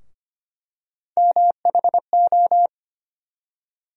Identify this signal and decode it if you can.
Morse code, heard as a keyed tone, about 25 wpm, 700 Hz. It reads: MHO
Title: USA Call Sign Postfixes - 3 Letters 25wpm